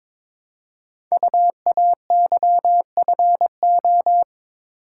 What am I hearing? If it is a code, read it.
Morse code, heard as a keyed tone, about 23 wpm, 700 Hz. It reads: UAYFO
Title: Code Groups - 5 Letters - 22wpm